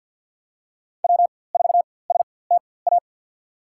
Morse code, heard as a keyed tone, about 52 wpm, spaced as for 23 wpm, 700 Hz. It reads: W4STA
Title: USA Call Signs - 5 Characters - 4x Character Spacing - 50wpm